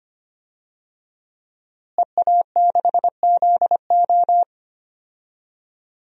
Morse code, heard as a keyed tone, about 25 wpm, 700 Hz. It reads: EA6ZO
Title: DXCC-100 Call Signs - 5 Characters - Repeated 3x - 25wpm